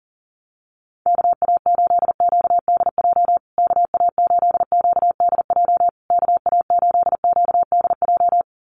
Morse code, heard as a keyed tone, about 37 wpm, 700 Hz. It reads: KA8QDJ KA8QDJ KA8QDJ
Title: USA Call Signs Repeated 3x 40wpm